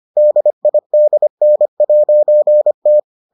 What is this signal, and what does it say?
Morse code, heard as a keyed tone, about 25 wpm, 600 Hz. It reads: DIDN'T